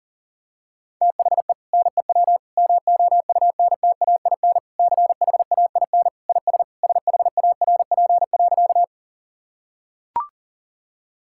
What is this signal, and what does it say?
Morse code, heard as a keyed tone, about 40 wpm, 700 Hz. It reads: THE NEW MOUNTAIN CHAIN IS SHARP.
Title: Sentences from Top 1600 Words 40wpm